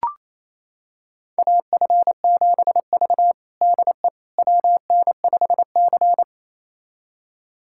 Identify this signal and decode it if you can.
Morse code, heard as a keyed tone, about 28 wpm, 700 Hz. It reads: AF7V DE WN5C
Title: USA Callsign DE Callsign - 4 Characters - 28wpm